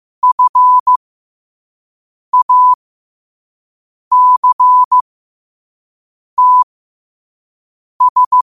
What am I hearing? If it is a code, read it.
Morse code, heard as a keyed tone, about 15 wpm, spaced as for 5 wpm, 1000 Hz. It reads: FACTS
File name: qotdfiles / 5wpm